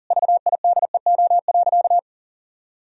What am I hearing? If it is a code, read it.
Morse code, heard as a keyed tone, about 40 wpm, 700 Hz. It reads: VIDEO.
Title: Sentences from Top 1000 Words - Review Part 4 - 40wpm